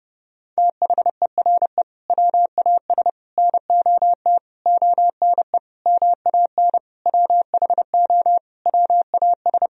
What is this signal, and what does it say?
Morse code, heard as a keyed tone, about 30 wpm, 700 Hz. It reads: THERE WAS NOT ONE MAN WHO WAS
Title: Sentences from Top 1000 Words - Review Part 2 - 30wpm